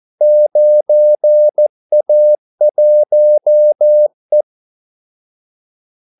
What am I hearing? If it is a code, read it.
Morse code, heard as a keyed tone, about 14 wpm, 600 Hz. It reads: 9A1E
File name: Callsigns-14wpm